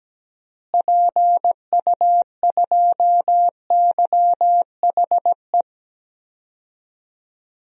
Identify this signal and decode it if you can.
Morse code, heard as a keyed tone, about 17 wpm, 700 Hz. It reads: PU2YHE
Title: DXCC-100 Call Signs - 6 Characters - Repeated 2x - 17wpm